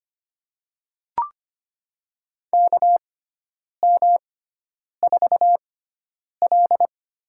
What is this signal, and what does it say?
Morse code, heard as a keyed tone, about 25 wpm, spaced as for 9 wpm, 700 Hz. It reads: KM4L
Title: USA Call Signs - 6x Character Spacing - 25wpm